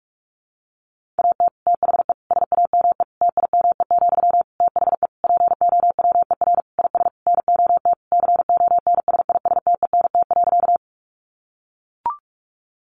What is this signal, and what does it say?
Morse code, heard as a keyed tone, about 44 wpm, 700 Hz. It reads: AT THE SAME TIME, THE POWER IS NOT CONSISTENT.
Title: Sentences from Top 1500 Words 45wpm